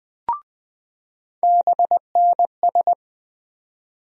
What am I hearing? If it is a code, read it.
Morse code, heard as a keyed tone, about 20 wpm, 700 Hz. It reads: BNS
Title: USA Call Sign Postfixes - 3 Letters 20wpm